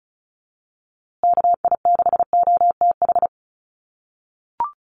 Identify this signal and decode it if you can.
Morse code, heard as a keyed tone, about 34 wpm, 700 Hz. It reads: KI6OTH
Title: USA Call Signs Repeated 2x 35wpm